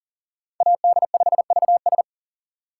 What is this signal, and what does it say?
Morse code, heard as a keyed tone, about 40 wpm, 700 Hz. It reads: AD5VS
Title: USA Call Signs - 4x Character Spacing - 40wpm